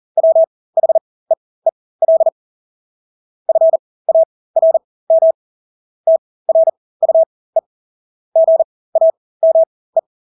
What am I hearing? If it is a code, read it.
Morse code, heard as a keyed tone, about 40 wpm, spaced as for 20 wpm, 650 Hz. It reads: WHEEL FARM TRUE GAME
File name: Engelske ord 40-20 wpm